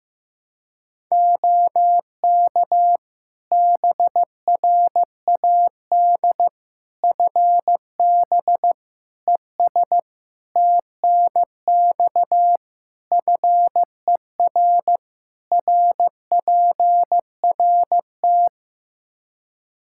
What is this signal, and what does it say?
Morse code, heard as a keyed tone, about 15 wpm, 700 Hz. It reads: OK BRAD FB ES TNX FER RPRT
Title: K1USN SST QSO Practice 15wpm